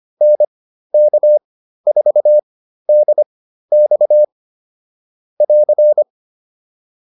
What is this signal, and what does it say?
Morse code, heard as a keyed tone, about 25 wpm, spaced as for 13 wpm, 600 Hz. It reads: NK4DX <AR>